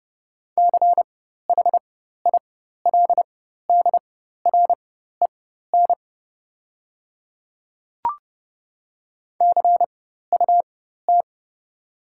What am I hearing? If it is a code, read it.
Morse code, heard as a keyed tone, about 30 wpm, spaced as for 14 wpm, 700 Hz. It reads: CHILDREN CUT
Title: 4x Character Spacing - Top 300 Words - 30wpm